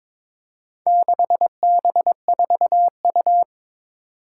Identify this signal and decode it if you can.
Morse code, heard as a keyed tone, about 22 wpm, 700 Hz. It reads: 6B4U
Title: Four Challenging Letter-Number Combinations 22wpm